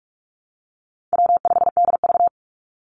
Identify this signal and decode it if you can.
Morse code, heard as a keyed tone, about 44 wpm, 700 Hz. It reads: W5DV
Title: Easy Call Signs Repeated 3x - 45wpm